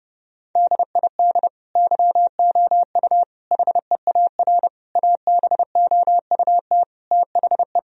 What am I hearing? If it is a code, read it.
Morse code, heard as a keyed tone, about 32 wpm, 700 Hz. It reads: DID YOU HEAR ABOUT THE
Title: Sentences from Top 1000 Words - Review Part 4 - 30wpm